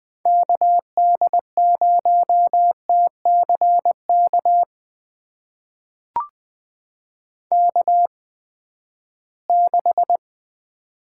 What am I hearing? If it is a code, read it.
Morse code, heard as a keyed tone, about 20 wpm, spaced as for 14 wpm, 700 Hz. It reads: KD0TCK K6
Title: USA Call Signs - 8x Character Spacing - 20wpm